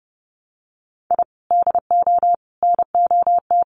Morse code, heard as a keyed tone, about 30 wpm, 700 Hz. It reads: I DO NOT
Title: Sets of 4 Words 30wpm